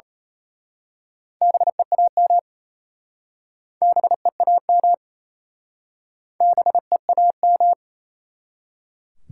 Morse code, heard as a keyed tone, about 35 wpm, 700 Hz. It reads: BEAM BEAM BEAM
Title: Speed-Racing Top 90 QSO Elements 28wpm